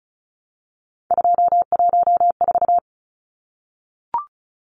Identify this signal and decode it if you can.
Morse code, heard as a keyed tone, about 34 wpm, 700 Hz. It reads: 214
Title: Numbers - 3-Digit 35wpm